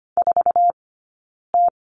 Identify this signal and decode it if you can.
Morse code, heard as a keyed tone, about 25 wpm, spaced as for 9 wpm, 690 Hz. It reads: 4T
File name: Lesson 15d - Number 5 - Two Characters 25wpm